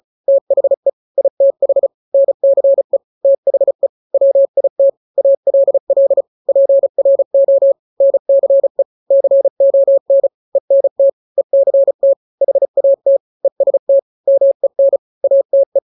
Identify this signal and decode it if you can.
Morse code, heard as a keyed tone, about 34 wpm, 550 Hz. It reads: THE ITH NCE THE WIT ALL PRO NCE CON ENT ECT HAT EST MEN ATE
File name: 3-grams 35 30 WPM